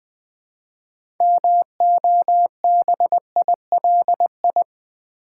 Morse code, heard as a keyed tone, about 20 wpm, 700 Hz. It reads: MOBILI
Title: Build A Word - 3 Letters At A Time - 8-Letter Words - 20wpm